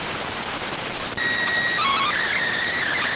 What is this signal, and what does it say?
SSTV (slow-scan television), PD 290.